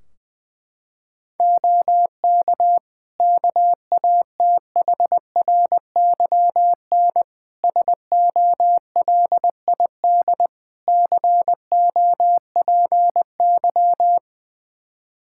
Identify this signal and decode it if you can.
Morse code, heard as a keyed tone, about 20 wpm, 700 Hz. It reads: OK KATHRYN SOLID COPY
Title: K1USN SST QSO Practice 20wpm